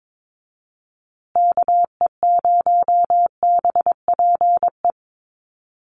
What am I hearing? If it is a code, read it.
Morse code, heard as a keyed tone, about 22 wpm, 700 Hz. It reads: KE0BPE
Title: USA Call Signs Repeated 3x 22wpm